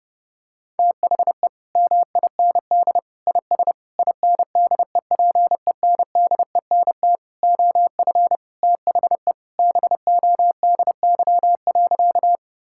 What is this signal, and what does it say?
Morse code, heard as a keyed tone, about 30 wpm, 700 Hz. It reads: THE MIND IS INDEPENDENT OF THE BODY.
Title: Sentences from Top 1000 Words - Review Part 6 - 30wpm